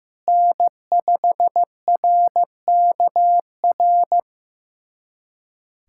Morse code, heard as a keyed tone, about 15 wpm, 700 Hz. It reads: N5RKR
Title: USA Callsign DE Callsign - 5 Characters - 15wpm